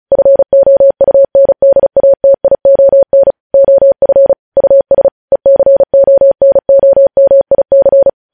Morse code, heard as a keyed tone, about 36 wpm, 550 Hz. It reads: FOUNDATION OF US ECONOMIC
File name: podcast / News headlines-1x-35x35